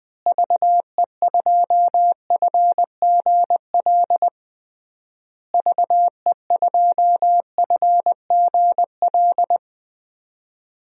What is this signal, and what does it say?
Morse code, heard as a keyed tone, about 20 wpm, 700 Hz. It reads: VE2FGL VE2FGL
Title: DXCC-100 Call Signs - 6 Characters - Repeated 3x - 20wpm